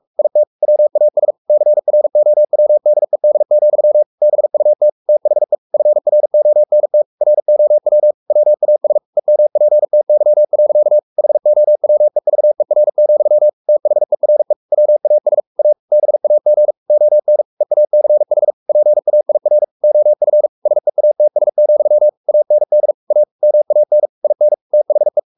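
Morse code, heard as a keyed tone, about 44 wpm, 600 Hz. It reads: IT WAS CROWDED, BUT THE FRONT ROW WAS EMPTY. HOWEVER, THERE WAS A BAG ON EACH PAIR OF SEATS, AND A MAN IN THE
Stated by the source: Morse code at 43 wpm sending Ken Follett's novel Never in English, from CW med Gnister.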